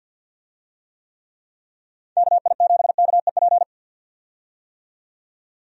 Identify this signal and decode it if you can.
Morse code, heard as a keyed tone, about 51 wpm, 700 Hz. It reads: KI7KEP
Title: USA Call Signs - 6 Characters - 50wpm